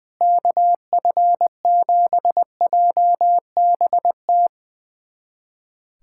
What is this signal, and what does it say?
Morse code, heard as a keyed tone, about 20 wpm, 700 Hz. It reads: KF7JBT
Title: USA Callsign DE Callsign 20wpm